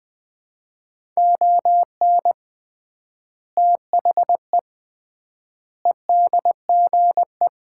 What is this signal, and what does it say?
Morse code, heard as a keyed tone, about 21 wpm, 700 Hz. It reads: ON THE EDGE
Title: 3x Word-Spacing - Sets of 5 Words - 20wpm